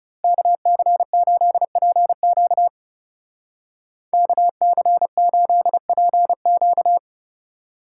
Morse code, heard as a keyed tone, about 35 wpm, 700 Hz. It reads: KC8PQ KC8PQ
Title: Speed Racing - USA Call Signs 30wpm